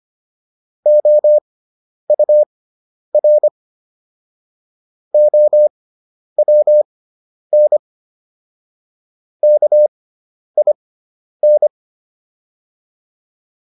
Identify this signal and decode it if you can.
Morse code, heard as a keyed tone, about 25 wpm, spaced as for 10 wpm, 600 Hz. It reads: OUR OWN KIN